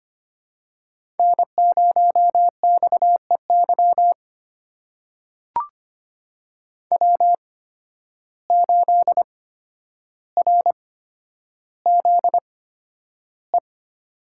Morse code, heard as a keyed tone, about 25 wpm, spaced as for 11 wpm, 700 Hz. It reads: N0XEY W8RZE
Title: USA Call Signs - 5 Characters - 8x Character Spacing - 25wpm